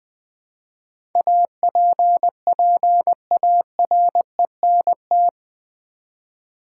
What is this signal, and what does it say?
Morse code, heard as a keyed tone, about 20 wpm, 700 Hz. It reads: APPARENT
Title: Build A Word - 3 Letters At A Time - 8-Letter Words - 20wpm